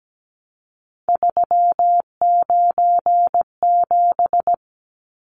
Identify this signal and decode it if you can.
Morse code, heard as a keyed tone, about 17 wpm, 700 Hz. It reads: 397
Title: Numbers - 3-Digit 17wpm